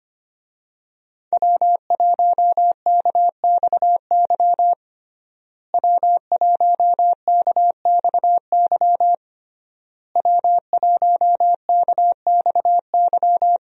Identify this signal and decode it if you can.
Morse code, heard as a keyed tone, about 25 wpm, 700 Hz. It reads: W1KXY W1KXY W1KXY
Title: USA Call Signs - 5 Characters - Repeated 3x - 25wpm